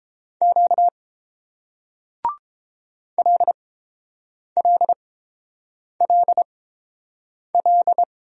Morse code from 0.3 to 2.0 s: Q

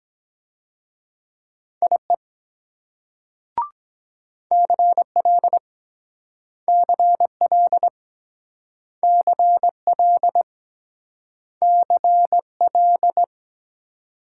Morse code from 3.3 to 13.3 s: CL CL CL CL

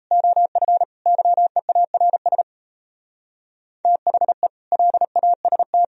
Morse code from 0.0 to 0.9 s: OF

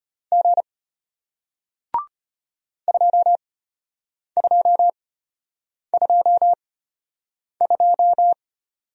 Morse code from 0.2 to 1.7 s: G

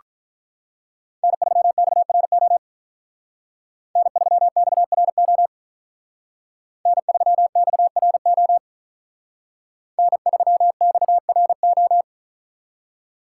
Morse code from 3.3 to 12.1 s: N3XRO N3XRO N3XRO

